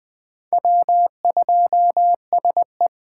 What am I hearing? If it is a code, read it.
Morse code, heard as a keyed tone, about 20 wpm, 700 Hz. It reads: W2SE
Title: 2023 SOTA Chaser Call Signs - 20wpm